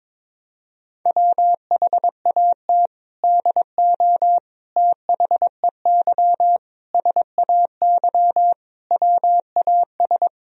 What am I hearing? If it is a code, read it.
Morse code, heard as a keyed tone, about 22 wpm, 700 Hz. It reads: WHAT DO THEY SAY WAS
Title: Sentences from Top 1000 Words - Review Part 1 - 22wpm